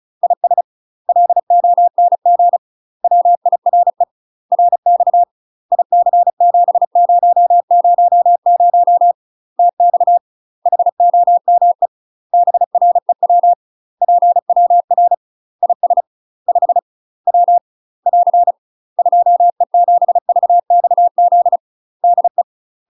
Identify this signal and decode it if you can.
Morse code, heard as a keyed tone, about 35 wpm, 700 Hz. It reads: IS LONG WIRE RX IC7000 TX HOME BREW PWR IS 5 W <AR> 2E7VXZ DE
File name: UK MASTER QSOs TRAINING / 35WPM / UK MASTER QSOs 0023